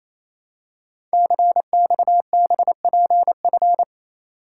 Morse code, heard as a keyed tone, about 27 wpm, 700 Hz. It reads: CXBPF
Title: Code Groups - 5 Letters - 28wpm